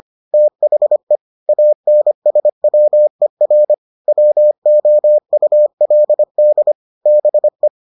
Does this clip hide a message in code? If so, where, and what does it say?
Yes: Morse code from 0.3 to 7.7 s, heard as a keyed tone, about 25 wpm, 600 Hz. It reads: THE ANSWER WOULD BE